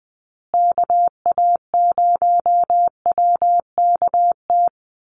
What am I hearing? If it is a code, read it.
Morse code, heard as a keyed tone, about 20 wpm, 700 Hz. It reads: KA0WKT
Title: USA Call Signs Repeated 2x 20wpm